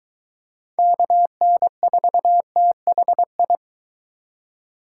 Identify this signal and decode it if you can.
Morse code, heard as a keyed tone, about 23 wpm, 700 Hz. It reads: KN4THI